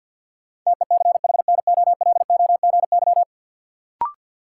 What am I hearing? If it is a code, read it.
Morse code, heard as a keyed tone, about 51 wpm, 700 Hz. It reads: TECHNOLOGY
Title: Sentences from Top 1200 Words - New Word Drill Alt - 50wpm